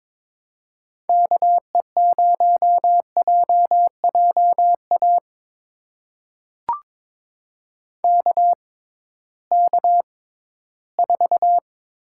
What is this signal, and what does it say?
Morse code, heard as a keyed tone, about 22 wpm, spaced as for 14 wpm, 700 Hz. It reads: KE0JJA KK4
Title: USA Call Signs - 6 Characters - 6x Character Spacing - 22wpm